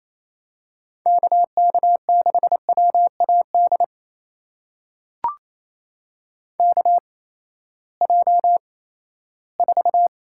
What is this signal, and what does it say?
Morse code, heard as a keyed tone, about 27 wpm, spaced as for 16 wpm, 700 Hz. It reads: KK6WAD KJ4
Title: USA Call Signs - 6 Characters - 8x Character Spacing - 28wpm